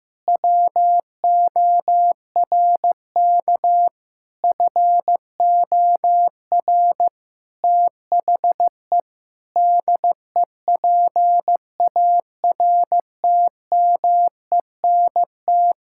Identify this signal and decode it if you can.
Morse code, heard as a keyed tone, about 15 wpm, 700 Hz. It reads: WORK FOR THE DEPARTMENT
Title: Sentences from Top 1000 Words - Review Part 5 - 15wpm